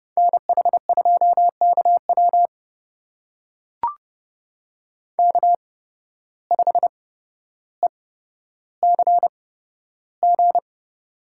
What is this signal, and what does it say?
Morse code, heard as a keyed tone, about 30 wpm, spaced as for 13 wpm, 700 Hz. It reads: NH2KW K5ECG
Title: USA Call Signs - 5 Characters - 8x Character Spacing - 30wpm